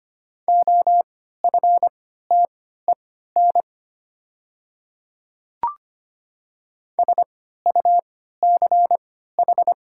Morse code, heard as a keyed tone, about 25 wpm, spaced as for 14 wpm, 690 Hz. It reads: OFTEN SUCH
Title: 3x Character Spacing - Top 500 Words - 25wpm